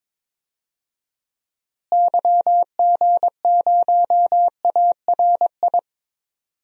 Morse code, heard as a keyed tone, about 22 wpm, 700 Hz. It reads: YG0ARI
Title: DXCC-100 Call Signs - 6 Characters - Repeated 3x - 22wpm